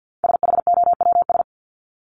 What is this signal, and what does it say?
Morse code, heard as a keyed tone, about 50 wpm, 700 Hz. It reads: SHOWS
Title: ES:S Suffix - Top 100 Words with Base Words 50wpm